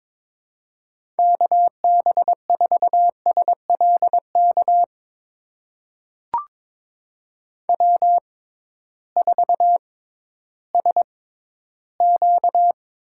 Morse code, heard as a keyed tone, about 22 wpm, spaced as for 13 wpm, 700 Hz. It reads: KB4SLK W4SQ
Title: USA Call Signs - 6x Character Spacing - 22wpm